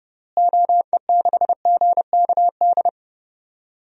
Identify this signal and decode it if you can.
Morse code, heard as a keyed tone, about 30 wpm, 700 Hz. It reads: OE6GKD
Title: DXCC-100 Call Signs - 6 Characters - Repeated 2x - 30wpm